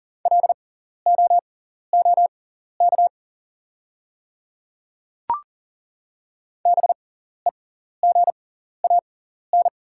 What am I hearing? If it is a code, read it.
Morse code, heard as a keyed tone, about 40 wpm, spaced as for 14 wpm, 700 Hz. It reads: LOOK BEGAN
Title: 6x Character Spacing - Top 300 Words - 40wpm